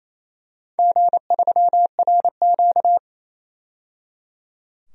Morse code, heard as a keyed tone, about 28 wpm, 700 Hz. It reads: G3RQ
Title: DXCC-100 Call Signs - 4 Characters - 28wpm